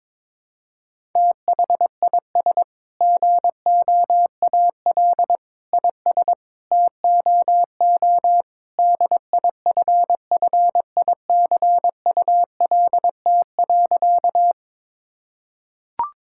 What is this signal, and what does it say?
Morse code, heard as a keyed tone, about 22 wpm, 700 Hz. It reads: THIS GOAL IS TOO DIFFICULT.
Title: Sentences from Top 1000 Words - Review Part 5 - 22wpm